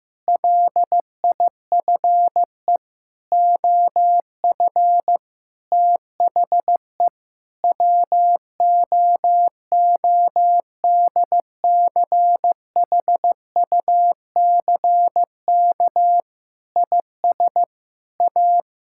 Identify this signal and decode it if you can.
Morse code, heard as a keyed tone, about 15 wpm, 700 Hz. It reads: LIFE OF THE WOODCHUCK IS A
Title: Pangram Challenge v2 - 15wpm